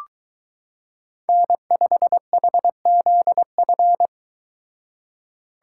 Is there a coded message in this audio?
Yes: Morse code from 1.3 to 4.1 s, heard as a keyed tone, about 22 wpm, 700 Hz. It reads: N5HZF